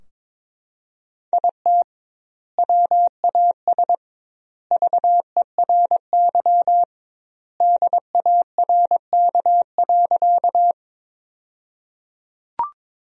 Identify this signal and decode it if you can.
Morse code, heard as a keyed tone, about 22 wpm, 700 Hz. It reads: IT WAS VERY DARK.
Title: 2x Word-Spacing - Sentences from Top 500 Words - 22wpm